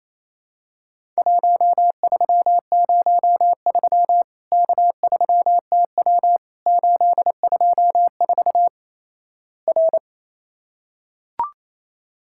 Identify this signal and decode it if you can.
Morse code, heard as a keyed tone, about 28 wpm, 700 Hz. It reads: 1303 K3TW 824 R
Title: Worked All Europe DX Contest - QTC Exchange - 28wpm